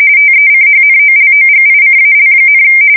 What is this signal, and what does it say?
RTTY, about 45.45 baud. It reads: TECHNOLOGICAL,
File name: fsk170 / USS-New-Jersey